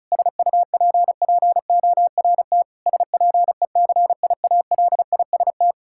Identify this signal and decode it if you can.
Morse code, heard as a keyed tone, about 36 wpm, 700 Hz. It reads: SUPPORT SPECIALIST
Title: Practice QSO - Exchange 3 - More Info - 35wpm